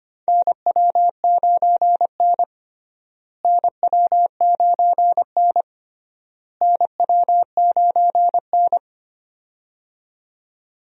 Morse code, heard as a keyed tone, about 25 wpm, 700 Hz. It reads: NW9N NW9N NW9N
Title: USA Call Signs - 4 Characters - Repeated 3x - 25wpm